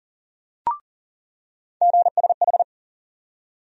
Morse code, heard as a keyed tone, about 37 wpm, 700 Hz. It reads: GSH
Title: USA Call Sign Postfixes - 3 Letters 40wpm